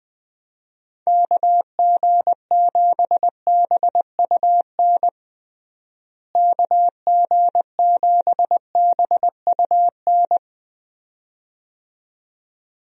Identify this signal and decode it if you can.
Morse code, heard as a keyed tone, about 20 wpm, 700 Hz. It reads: KG7BUN KG7BUN
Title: USA Call Signs - 6 Characters - Repeated 2x - 20wpm